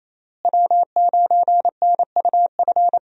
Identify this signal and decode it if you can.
Morse code, heard as a keyed tone, about 28 wpm, 700 Hz. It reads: W9NUF
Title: USA Call Signs - 5 Characters - Repeated 3x - 28wpm